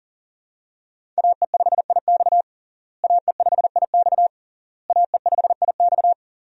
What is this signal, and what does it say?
Morse code, heard as a keyed tone, about 40 wpm, 700 Hz. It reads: AE5IX AE5IX AE5IX
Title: USA Call Signs - 5 Characters - Repeated 3x - 40wpm